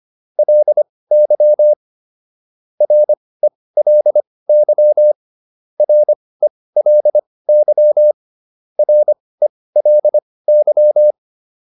Morse code, heard as a keyed tone, about 25 wpm, spaced as for 18 wpm, 600 Hz. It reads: LY RELY RELY RELY